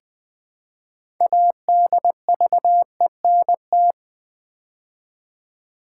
Morse code, heard as a keyed tone, about 20 wpm, 700 Hz. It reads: ADVENT
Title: Build A Word - 3 Letters At A Time - 9-Letter Words - 20wpm